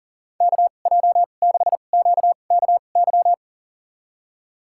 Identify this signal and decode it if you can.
Morse code, heard as a keyed tone, about 40 wpm, spaced as for 29 wpm, 700 Hz. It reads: KJ6QKY
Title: USA Call Signs - 6 Characters - 2x Character Spacing - 40wpm